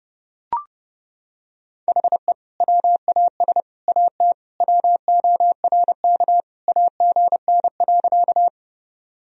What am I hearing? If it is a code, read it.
Morse code, heard as a keyed tone, about 30 wpm, 700 Hz. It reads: HE WAS AT WORK AGN.
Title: Sentences from Top 200 Words and Abbreviations 30wpm